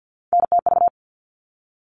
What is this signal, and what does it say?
Morse code, heard as a keyed tone, about 48 wpm, 700 Hz. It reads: NTV
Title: All Cut Numbers - 3-Digit 50wpm